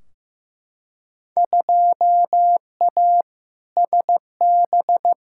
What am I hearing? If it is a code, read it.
Morse code, heard as a keyed tone, about 15 wpm, 700 Hz. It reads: 2A SB
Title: Field Day Exchanges - 15wpm